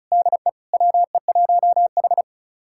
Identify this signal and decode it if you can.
Morse code, heard as a keyed tone, about 36 wpm, 700 Hz. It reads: DE WE1H
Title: USA Callsign DE Callsign - 4 Characters - 35wpm